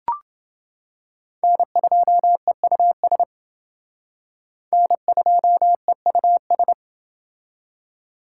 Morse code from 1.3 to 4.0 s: N2EUS

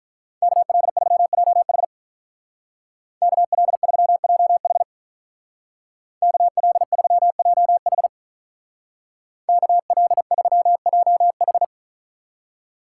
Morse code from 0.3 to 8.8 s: KL3JH KL3JH KL3JH

